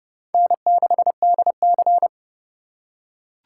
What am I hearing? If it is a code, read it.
Morse code, heard as a keyed tone, about 29 wpm, 700 Hz. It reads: N6DC